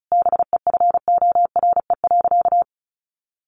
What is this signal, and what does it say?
Morse code, heard as a keyed tone, about 34 wpm, 700 Hz. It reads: BEFORE.
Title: Sentences from Top 600 Words 35wpm